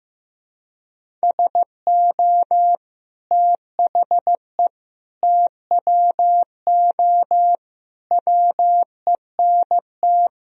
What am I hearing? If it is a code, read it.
Morse code, heard as a keyed tone, about 14 wpm, 700 Hz. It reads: SO THE TWO WENT
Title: Sentences from Top 1000 Words - Review Part 1 - 15wpm